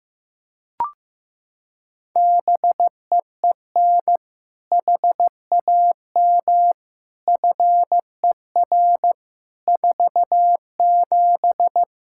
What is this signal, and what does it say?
Morse code, heard as a keyed tone, about 16 wpm, 700 Hz. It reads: BEEN HAM FER 47
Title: Practice QSO - Exchange 3 - More Info - 15wpm